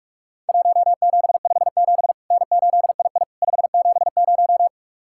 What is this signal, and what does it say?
Morse code, heard as a keyed tone, about 44 wpm, 700 Hz. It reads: 1757 N8II 570
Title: Worked All Europe DX Contest - QTC Exchange - 45wpm